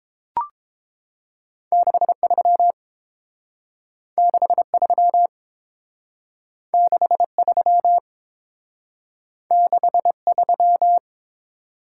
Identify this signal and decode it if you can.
Morse code, heard as a keyed tone, about 29 wpm, 700 Hz. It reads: 63 63 63 63